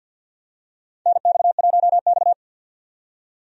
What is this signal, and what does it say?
Morse code, heard as a keyed tone, about 52 wpm, 700 Hz. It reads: NX1X
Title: USA Call Signs - 4 Characters - 50wpm